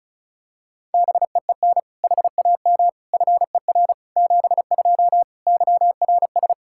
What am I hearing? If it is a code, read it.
Morse code, heard as a keyed tone, about 34 wpm, 700 Hz. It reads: BEEN HAM FER 72 YRS
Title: K1USN SST QSO Practice 35wpm